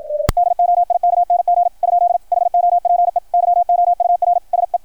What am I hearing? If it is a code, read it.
Morse code, heard as a keyed tone, about 53 wpm, 700 Hz. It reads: DOIKNG ? HOPE YOUR SE
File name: archived / CRSnet